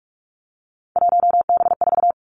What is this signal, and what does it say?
Morse code, heard as a keyed tone, about 46 wpm, 700 Hz. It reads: 1B4
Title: Three Challenging Letter-Number Combinations 45wpm